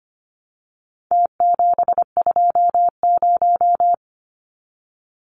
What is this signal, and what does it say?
Morse code, heard as a keyed tone, about 25 wpm, 700 Hz. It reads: T720
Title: Numbers - Contest Style - 4-Digit 25wpm